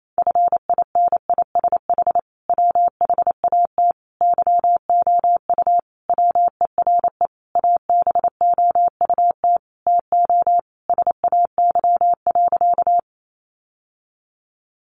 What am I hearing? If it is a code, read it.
Morse code, heard as a keyed tone, about 28 wpm, 700 Hz. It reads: FINISH WHAT YOU WERE ABOUT TO SAY.
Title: Sentences from Top 1200 Words 28wpm